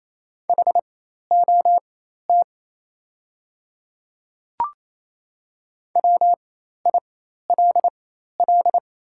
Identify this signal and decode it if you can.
Morse code, heard as a keyed tone, about 28 wpm, spaced as for 13 wpm, 700 Hz. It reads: HOT WILL
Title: 4x Character Spacing - Top 100 Words - 28wpm